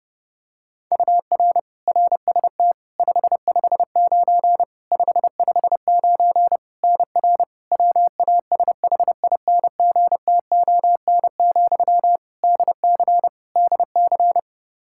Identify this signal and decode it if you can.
Morse code, heard as a keyed tone, about 30 wpm, 700 Hz. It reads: UR RST 559 559 NR WASHINGTON, DC DC
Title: Practice QSO - Exchange 1 - Beginning - 30wpm